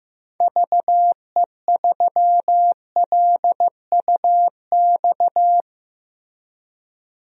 Morse code, heard as a keyed tone, about 15 wpm, 700 Hz. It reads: VE3LUX
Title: DXCC-100 Call Signs - 6 Characters - Repeated 3x - 15wpm